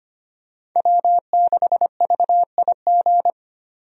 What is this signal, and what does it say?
Morse code, heard as a keyed tone, about 25 wpm, 700 Hz. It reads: W6VIG
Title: USA Call Signs - 5 Characters - 25wpm